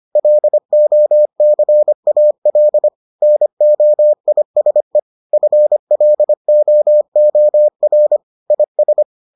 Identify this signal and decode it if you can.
Morse code, heard as a keyed tone, about 25 wpm, 600 Hz. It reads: LOCAL NOISE FLOOR IS